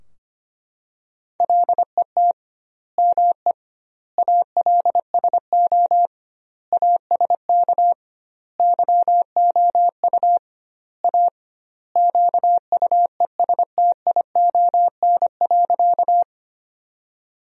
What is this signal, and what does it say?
Morse code, heard as a keyed tone, about 25 wpm, 700 Hz. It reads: LET ME ALSO ASK YOU A QUESTION.
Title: 2x Word-Spacing - Sentences from Top 400 Words - 25wpm